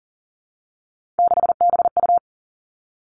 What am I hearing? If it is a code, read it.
Morse code, heard as a keyed tone, about 40 wpm, 700 Hz. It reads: 6BU
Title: Three Challenging Letter-Number Combinations 40wpm